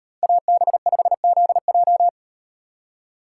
Morse code, heard as a keyed tone, about 38 wpm, 700 Hz. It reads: AB5ZJ